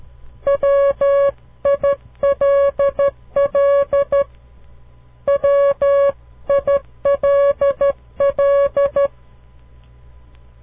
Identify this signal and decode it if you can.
Morse code, heard as a keyed tone, about 12 wpm, 560 Hz. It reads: WILL WILL
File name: CW Learning Programs / farnsworth / unit 2